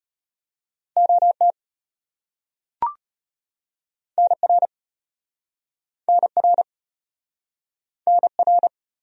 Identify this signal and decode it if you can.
Morse code, heard as a keyed tone, about 35 wpm, 700 Hz. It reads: OT NR NR NR